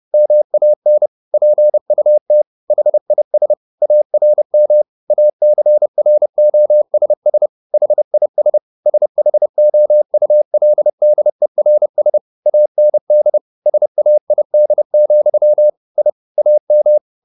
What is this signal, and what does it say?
Morse code, heard as a keyed tone, about 30 wpm, 600 Hz. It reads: MAN PUT HIS ARM ACROSS HIS SHOULDERS AND SAID, I AM
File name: the old man and the sea / the old man and the sea 1 30wpm